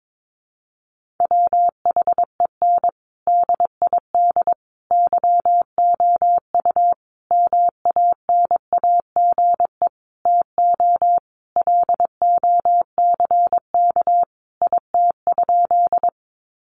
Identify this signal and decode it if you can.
Morse code, heard as a keyed tone, about 22 wpm, 700 Hz. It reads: WHEN DID YOU MANAGE TO LOCK IT?
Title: Sentences from Top 1400 Words 22wpm